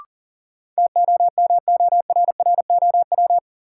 Morse code, heard as a keyed tone, about 39 wpm, 700 Hz. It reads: TOMORROW
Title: Sentences from Top 1100 Words - New Word Drill Alt - 40wpm